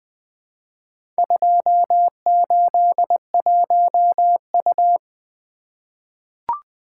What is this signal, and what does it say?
Morse code, heard as a keyed tone, about 20 wpm, 700 Hz. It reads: 281U
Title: Four Letter-Number Combinations - 20wpm